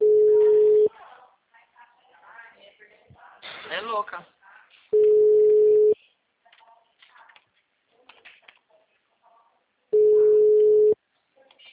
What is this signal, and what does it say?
Ringback tone (Europe, 425 Hz), with speech.